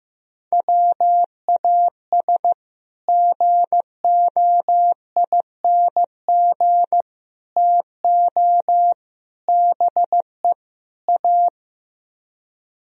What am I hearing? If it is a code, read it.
Morse code, heard as a keyed tone, about 15 wpm, 700 Hz. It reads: WAS GOING TO BE A
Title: Speed-Racing - Sets of 5 Words 15wpm